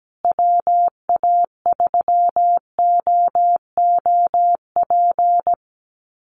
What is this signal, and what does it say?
Morse code, heard as a keyed tone, about 17 wpm, 700 Hz. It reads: WA3OOP
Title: USA Call Signs Repeated 3x 17wpm